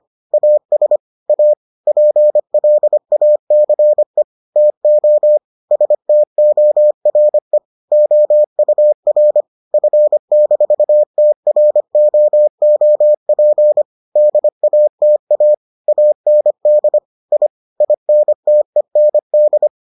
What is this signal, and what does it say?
Morse code, heard as a keyed tone, about 25 wpm, 600 Hz. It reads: AS A PLACE TO STORE OUR F-TROOP DATA AND I INTEND